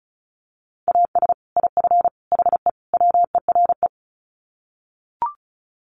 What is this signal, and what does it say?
Morse code, heard as a keyed tone, about 34 wpm, 700 Hz. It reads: AS IF HE WERE
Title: Sets of 4 Words 35wpm